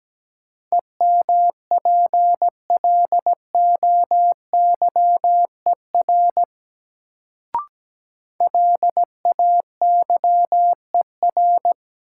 Morse code, heard as a keyed tone, about 17 wpm, 700 Hz. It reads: EMPLOYER LAYER
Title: Sentences from Top 1600 Words - New Word Drill Alt - 17wpm